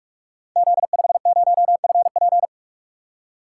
Morse code, heard as a keyed tone, about 44 wpm, 700 Hz. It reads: Z50FP
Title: Code Groups - 5 Letter:Numbers - 45wpm